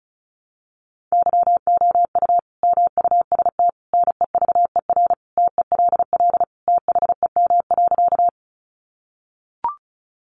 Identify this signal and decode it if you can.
Morse code, heard as a keyed tone, about 34 wpm, 700 Hz. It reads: YOU MUST NEVER TELL THEM.
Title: Sentences from Top 900 Words 35wpm